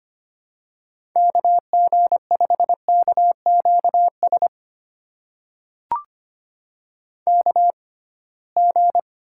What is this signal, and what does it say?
Morse code, heard as a keyed tone, about 25 wpm, spaced as for 19 wpm, 700 Hz. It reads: KG5KQS KG
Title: USA Call Signs - 6 Characters - 6x Character Spacing - 25wpm